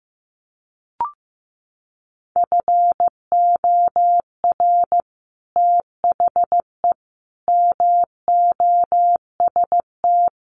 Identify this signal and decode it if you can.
Morse code, heard as a keyed tone, about 15 wpm, 700 Hz. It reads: FOR THE MOST
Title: Sets of 4 Words 15wpm